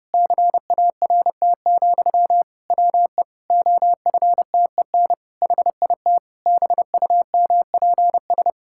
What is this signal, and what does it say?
Morse code, heard as a keyed tone, about 30 wpm, 700 Hz. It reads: CART, WE OFTEN HIT BUMPS
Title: Pangram Challenge v2 - 30wpm